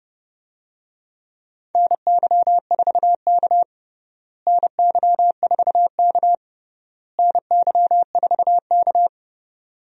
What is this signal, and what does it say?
Morse code, heard as a keyed tone, about 32 wpm, 700 Hz. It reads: NY4K NY4K NY4K
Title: USA Call Signs - 4 Characters - Repeated 3x - 30wpm